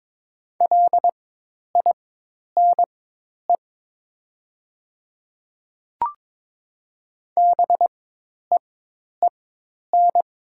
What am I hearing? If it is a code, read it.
Morse code, heard as a keyed tone, about 22 wpm, spaced as for 10 wpm, 700 Hz. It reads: LINE BEEN
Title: 4x Character Spacing - Top 200 Words - 22wpm